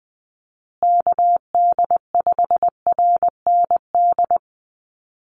Morse code, heard as a keyed tone, about 20 wpm, 700 Hz. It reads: KD5RND
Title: USA Call Signs Repeated 2x 20wpm